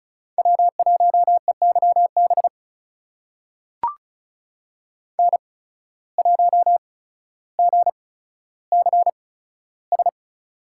Morse code, heard as a keyed tone, about 34 wpm, spaced as for 15 wpm, 700 Hz. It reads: W1EYB N1GCS
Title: USA Call Signs - 5 Characters - 8x Character Spacing - 35wpm